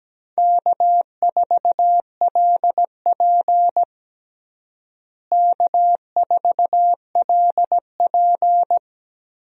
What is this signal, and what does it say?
Morse code, heard as a keyed tone, about 17 wpm, 700 Hz. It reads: K4LP K4LP
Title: USA Call Signs - 4 Characters - Repeated 2x - 17wpm